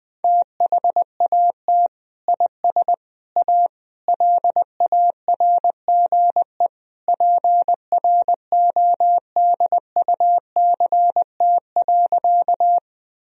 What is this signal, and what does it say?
Morse code, heard as a keyed tone, about 20 wpm, 700 Hz. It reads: THAT IS A LARGE PRODUCT.
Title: Sentences from Top 1000 Words - Review Part 2 - 20wpm